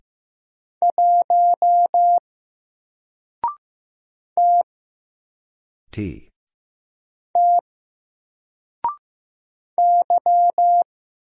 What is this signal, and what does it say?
Morse code, heard as a keyed tone, about 15 wpm, 700 Hz. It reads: 1 T T Y
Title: Single Letter-Number 15wpm